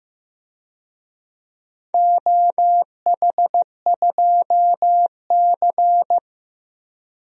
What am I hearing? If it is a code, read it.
Morse code, heard as a keyed tone, about 15 wpm, 700 Hz. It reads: OH2C
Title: DXCC-100 Call Signs - 4 Characters - 15wpm